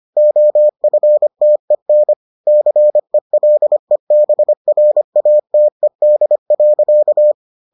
Morse code, heard as a keyed tone, about 25 wpm, 600 Hz. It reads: OFTEN CELEBRATED.